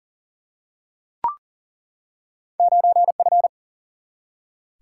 Morse code, heard as a keyed tone, about 38 wpm, 700 Hz. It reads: 9F
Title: Single-Letter Single-Number 40wpm